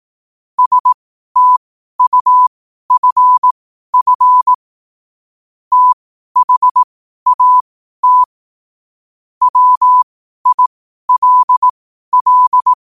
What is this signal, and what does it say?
Morse code, heard as a keyed tone, about 18 wpm, spaced as for 12 wpm, 1000 Hz. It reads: STUFF THAT WILL
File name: qotdfiles / 12wpm